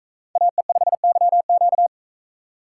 Morse code, heard as a keyed tone, about 42 wpm, 700 Hz. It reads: AE5YQ